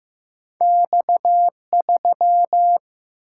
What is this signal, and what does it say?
Morse code, heard as a keyed tone, about 15 wpm, 700 Hz. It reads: X3
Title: Single-Letter Single-Number 15wpm